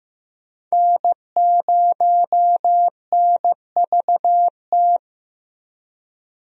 Morse code, heard as a keyed tone, about 15 wpm, 700 Hz. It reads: N0NVT
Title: USA Call Signs - 5 Characters - 15wpm